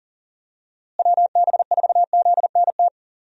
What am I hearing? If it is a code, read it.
Morse code, heard as a keyed tone, about 40 wpm, 700 Hz. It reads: WB4ZNT
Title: USA Call Signs - 6 Characters - Repeated 2x - 40wpm